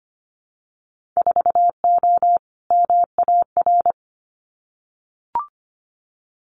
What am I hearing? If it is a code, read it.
Morse code, heard as a keyed tone, about 25 wpm, 700 Hz. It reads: 4O MAR
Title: Winter Field Day Exchanges Encore 25wpm